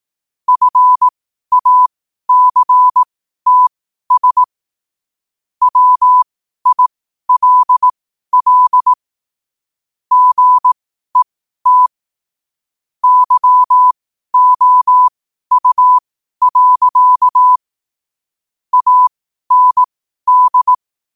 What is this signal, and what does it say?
Morse code, heard as a keyed tone, about 18 wpm, spaced as for 12 wpm, 1000 Hz. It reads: FACTS WILL GET YOU. AND
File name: qotdfiles / 12wpm